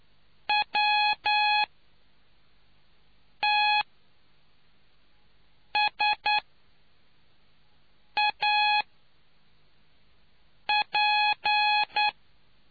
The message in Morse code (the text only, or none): WTSAP